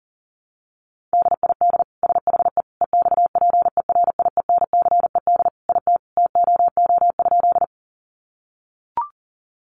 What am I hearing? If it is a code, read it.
Morse code, heard as a keyed tone, about 40 wpm, 700 Hz. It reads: DID SHE EXPERIENCED IT TOO?
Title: Sentences from Top 1100 Words 40wpm